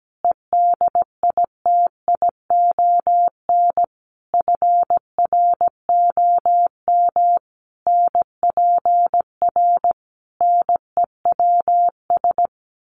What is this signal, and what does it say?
Morse code, heard as a keyed tone, about 17 wpm, 700 Hz. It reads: EDITION FROM NPR NEWS
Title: Sets of 5 Words Encore 17wpm